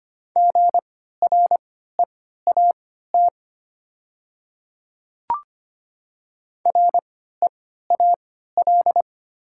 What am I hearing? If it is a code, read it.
Morse code, heard as a keyed tone, about 25 wpm, spaced as for 14 wpm, 700 Hz. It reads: GREAT REAL
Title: 3x Character Spacing - Top 500 Words - 25wpm